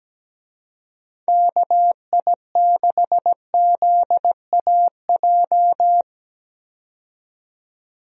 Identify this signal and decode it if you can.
Morse code, heard as a keyed tone, about 17 wpm, 700 Hz. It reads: KI6ZAJ
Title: USA Call Signs - 6 Characters - 17wpm